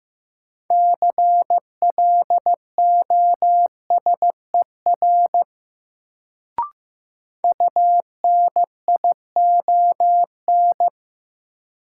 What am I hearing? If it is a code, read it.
Morse code, heard as a keyed tone, about 16 wpm, 700 Hz. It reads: CLOSER UNION
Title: Sets of 5 Words Encore - New Word Drill - 15wpm